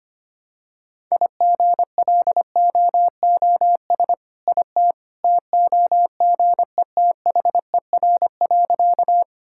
Morse code, heard as a keyed tone, about 25 wpm, 700 Hz. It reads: IGLOOS IT TOGETHER.
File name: practice / 25wpm / All-Age Appropriate Jokes - 25wpm